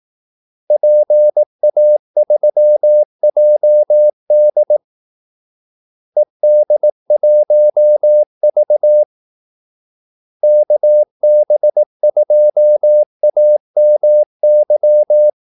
Morse code, heard as a keyed tone, about 18 wpm, 600 Hz. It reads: PA3JD ED1V KB2AMY
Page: Kaldesignaler 18 wpm